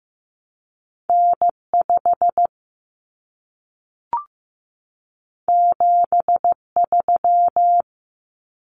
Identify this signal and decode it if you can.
Morse code, heard as a keyed tone, about 15 wpm, 700 Hz. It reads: N5 73
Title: Common Cut Numbers - 2-Digit 15wpm